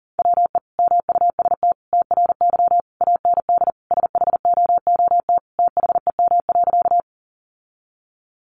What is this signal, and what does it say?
Morse code, heard as a keyed tone, about 40 wpm, 700 Hz. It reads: WE MUST TRY AND SHOOT THEM.
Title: Sentences from Top 1200 Words 40wpm